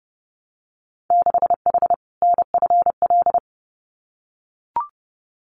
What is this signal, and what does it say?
Morse code, heard as a keyed tone, about 30 wpm, 700 Hz. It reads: 6H NFL
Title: Winter Field Day Exchanges Encore 30wpm